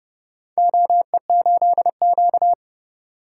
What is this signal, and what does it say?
Morse code, heard as a keyed tone, about 30 wpm, 700 Hz. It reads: OE8Q
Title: DXCC-100 Call Signs - 4 Characters - 30wpm